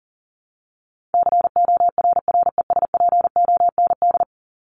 Morse code, heard as a keyed tone, about 40 wpm, 700 Hz. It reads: CORRESPOND
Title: ING Suffix - Top 100 Words with Base Words 40wpm